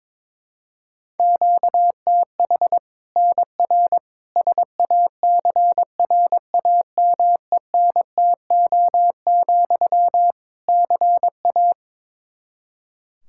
Morse code, heard as a keyed tone, about 21 wpm, 700 Hz. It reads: QTH NR SACRAMENTO, CA
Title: Practice QSO - Exchange 1 - Beginning - 22wpm